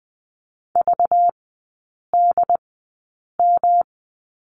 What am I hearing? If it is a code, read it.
Morse code, heard as a keyed tone, about 20 wpm, spaced as for 8 wpm, 700 Hz. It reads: VDM
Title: 191 Lesson 31e - Letter v - Three Characters 20wpm